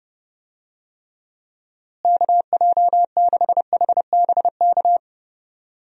Morse code, heard as a keyed tone, about 30 wpm, 700 Hz. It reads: KJ6HBK
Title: USA Call Signs - 6 Characters - Repeated 3x - 30wpm